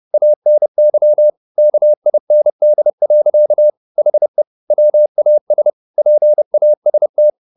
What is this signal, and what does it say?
Morse code, heard as a keyed tone, about 30 wpm, 600 Hz. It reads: ANY KIND. HE WAS PAST